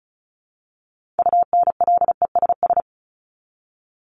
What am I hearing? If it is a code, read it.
Morse code, heard as a keyed tone, about 34 wpm, 700 Hz. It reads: UNLESS
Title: Top 751-1000 Words 35wpm